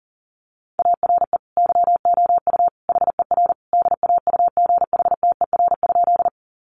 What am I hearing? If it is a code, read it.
Morse code, heard as a keyed tone, about 37 wpm, 700 Hz. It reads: ARE YOU HER DAUGHTER?
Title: Sentences from Top 1500 Words 40wpm